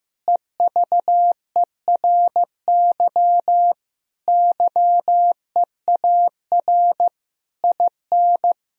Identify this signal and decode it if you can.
Morse code, heard as a keyed tone, about 15 wpm, 700 Hz. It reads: EVERY YEAR IN
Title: Sentences from Top 1000 Words - Review Part 1 - 15wpm